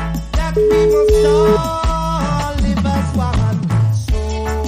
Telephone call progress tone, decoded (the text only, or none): ringback tone